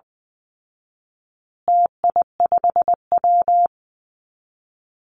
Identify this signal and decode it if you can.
Morse code, heard as a keyed tone, about 20 wpm, 700 Hz. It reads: TI5W
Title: Call Signs Repeated 2x - 20wpm